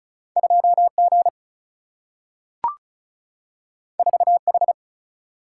Morse code, heard as a keyed tone, about 36 wpm, 700 Hz. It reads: 2 G 4 H
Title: Single-Letter Single-Number 35wpm